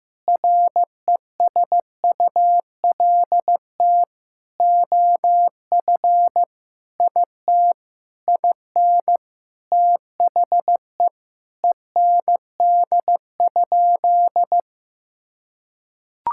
Morse code, heard as a keyed tone, about 15 wpm, 700 Hz. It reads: RESULT OF IT IN THE END?
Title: Sentences from Top 1000 Words - Review Part 5 - 15wpm